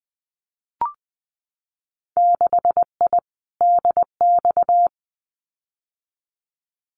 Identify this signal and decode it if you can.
Morse code, heard as a keyed tone, about 20 wpm, 700 Hz. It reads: 6I DX
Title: Winter Field Day Exchanges Encore 20wpm